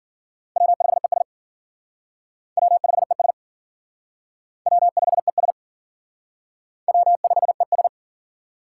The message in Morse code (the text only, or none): W5ES W5ES W5ES W5ES